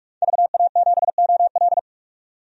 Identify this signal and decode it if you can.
Morse code, heard as a keyed tone, about 42 wpm, 700 Hz. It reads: VA7OL
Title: DXCC-100 Call Signs - 5 Characters - 45wpm